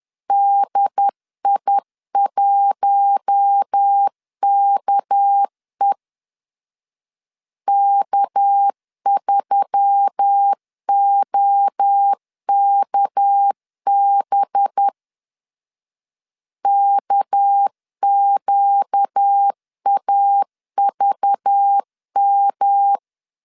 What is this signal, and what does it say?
Morse code, heard as a keyed tone, about 10 wpm, 800 Hz. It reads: DI1KE K3OKB KQAVM
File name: LEZIONI / 6 PKX0 / 6 PK 10wpm 2436